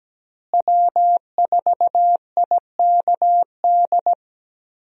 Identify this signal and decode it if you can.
Morse code, heard as a keyed tone, about 17 wpm, 700 Hz. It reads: W4IKD
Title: USA Call Signs - 5 Characters - Repeated 2x - 17wpm